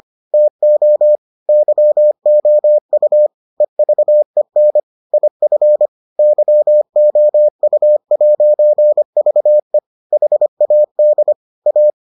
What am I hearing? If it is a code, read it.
Morse code, heard as a keyed tone, about 25 wpm, 600 Hz. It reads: TO YOU EVEN IF YOU'VE HAD A